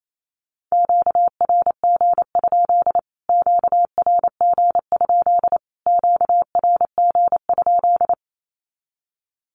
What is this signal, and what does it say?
Morse code, heard as a keyed tone, about 28 wpm, 700 Hz. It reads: QRG? QRG? QRG?
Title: Q-Codes Repeated 3x 28wpm